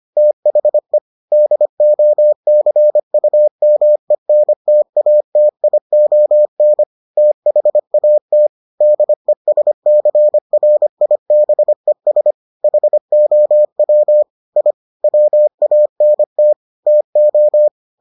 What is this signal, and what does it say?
Morse code, heard as a keyed tone, about 25 wpm, 600 Hz. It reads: THE DOCUMENTATION THAT DESCRIBES HOW I WANT TO